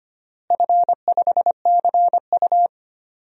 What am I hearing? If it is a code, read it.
Morse code, heard as a keyed tone, about 25 wpm, 700 Hz. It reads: F5CU
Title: DXCC-100 Call Signs - 4 Characters - Repeated 3x - 25wpm